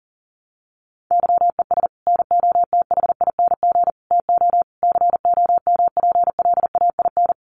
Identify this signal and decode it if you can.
Morse code, heard as a keyed tone, about 40 wpm, 700 Hz. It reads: YES NOTHING TO COMPLAIN
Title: Sentences from Top 1300 Words 40wpm